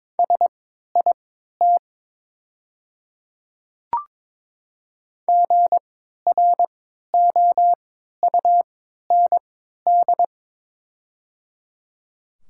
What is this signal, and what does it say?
Morse code, heard as a keyed tone, about 22 wpm, spaced as for 12 wpm, 700 Hz. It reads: SIT GROUND
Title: 3x Character Spacing - Top 500 Words - 22wpm